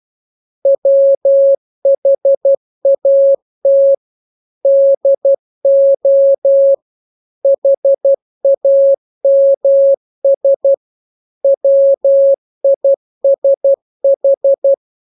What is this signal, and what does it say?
Morse code, heard as a keyed tone, about 12 wpm, 550 Hz. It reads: WHAT DO HAMS WISH
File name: jokes / joke 17 12 0000